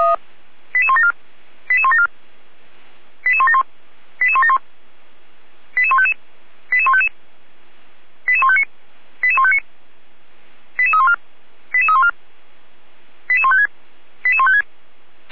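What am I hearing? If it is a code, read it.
Five-tone selective call (ZVEI), heard as quick runs of five tones, tones of about 70 ms. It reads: 88174 88174 88171 88171 88155 88155 88159 88159 88314 88314 88156 88156
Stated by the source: Five-tone ZVEI selective call received on an SDR, fire brigade Orlamünde (feuerwehr-orlamuende).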